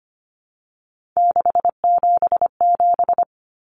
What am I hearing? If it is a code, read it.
Morse code, heard as a keyed tone, about 25 wpm, 700 Hz. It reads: 677
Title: Numbers - 3-Digit 25wpm